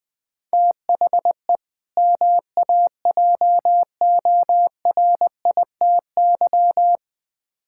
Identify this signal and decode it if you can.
Morse code, heard as a keyed tone, about 21 wpm, 700 Hz. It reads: THE MAJORITY
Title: Build A Phrase - 4-Word Phrases - 20wpm